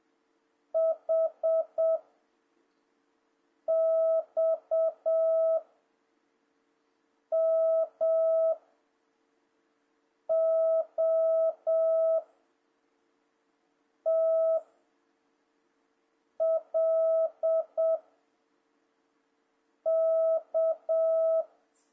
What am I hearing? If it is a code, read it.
Morse code, heard as a keyed tone, about 7 wpm, 650 Hz. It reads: H X M O T L K
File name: jh8cbh / 2026cw